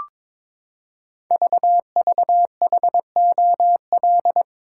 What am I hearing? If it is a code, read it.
Morse code, heard as a keyed tone, about 22 wpm, 690 Hz. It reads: VVHOL
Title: Code Groups - 5 Letters - 22wpm